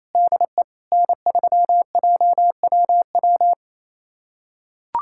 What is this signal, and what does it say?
Morse code, heard as a keyed tone, about 28 wpm, 700 Hz. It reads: DE N3JWW
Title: USA Callsign DE Callsign - 5 Characters - 28wpm